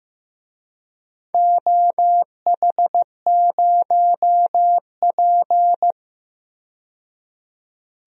Morse code, heard as a keyed tone, about 15 wpm, 700 Hz. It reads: OH0P
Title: DXCC-100 Call Signs - 4 Characters - 15wpm